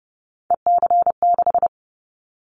Morse code, heard as a keyed tone, about 30 wpm, 700 Hz. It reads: EC6
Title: Callsign Prefixes for Top 100 DXCC - 30wpm